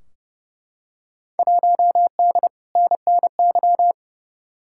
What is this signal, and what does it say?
Morse code, heard as a keyed tone, about 29 wpm, 700 Hz. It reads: 1D NNY
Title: Field Day Exchanges - 30wpm